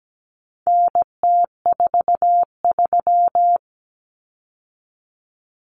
Morse code, heard as a keyed tone, about 17 wpm, 700 Hz. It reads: NT43
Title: Common Cut Numbers - 4-Digit 17wpm